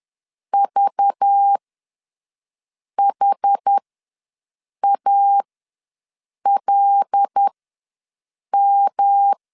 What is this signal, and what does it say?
Morse code, heard as a keyed tone, about 10 wpm, spaced as for 6 wpm, 800 Hz. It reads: VHALM
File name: LEZIONI / 10 SIGNS / 10C DOT ws 10wpm 4213